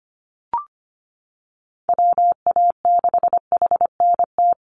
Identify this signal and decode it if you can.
Morse code, heard as a keyed tone, about 25 wpm, 700 Hz. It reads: WA6HNT
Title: USA Call Signs 25wpm